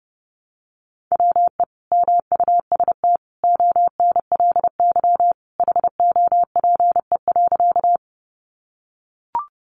Morse code, heard as a keyed tone, about 30 wpm, 700 Hz. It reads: WE MUST ONLY HOPE.
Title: Sentences from Top 1000 Words 30wpm